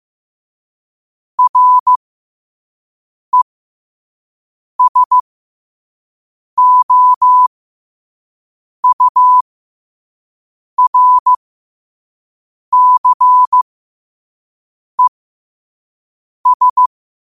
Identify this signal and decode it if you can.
Morse code, heard as a keyed tone, about 15 wpm, spaced as for 5 wpm, 1000 Hz. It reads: RESOURCES
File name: qotdfiles / 5wpm